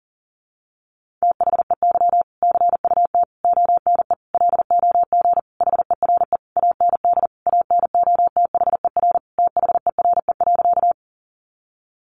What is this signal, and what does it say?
Morse code, heard as a keyed tone, about 40 wpm, 700 Hz. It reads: THEY CUT ONE LOG HERE AND ANOTHER THERE.
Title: Sentences from Top 1400 Words 40wpm